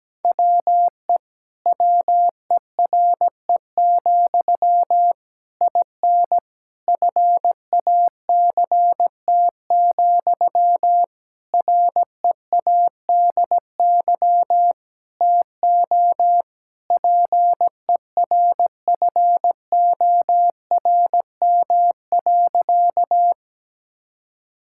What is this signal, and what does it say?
Morse code, heard as a keyed tone, about 17 wpm, 700 Hz. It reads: WE WERE, IN FACT, READY TO PERFORM.
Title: Sentences from Top 1000 Words - Review Part 5 - 17wpm